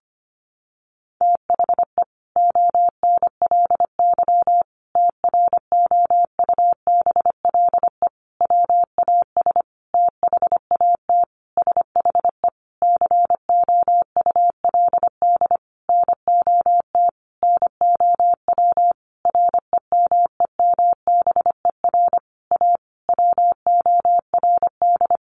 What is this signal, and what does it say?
Morse code, heard as a keyed tone, about 25 wpm, 700 Hz. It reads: THE ONLY TROUBLE WAS THAT SHE COULD NOT NOW REMEMBER A WORD
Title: Sentences from Top 1200 Words 25wpm